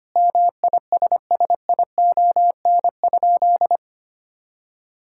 Morse code, heard as a keyed tone, about 25 wpm, 700 Hz. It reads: MISSION?
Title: Sentences from Top 1600 Words 25wpm